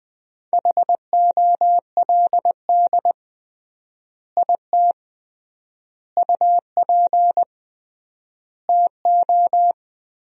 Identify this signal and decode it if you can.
Morse code, heard as a keyed tone, about 20 wpm, 700 Hz. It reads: HOLD IT UP TO
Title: 3x Word-Spacing - Sentences from Top 400 Words - 20wpm